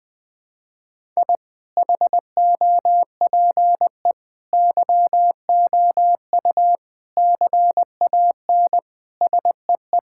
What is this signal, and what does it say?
Morse code, heard as a keyed tone, about 20 wpm, 700 Hz. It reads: I HOPE YOU CAN SEE
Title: Sentences with 9-Word Phrases - 20wpm